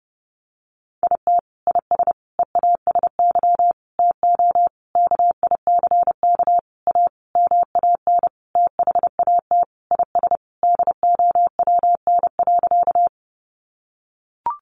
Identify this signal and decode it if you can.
Morse code, heard as a keyed tone, about 30 wpm, 700 Hz. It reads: IT IS EASY TO KICK A MAN THAT IS DOWN.
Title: Sentences from Top 1300 Words 30wpm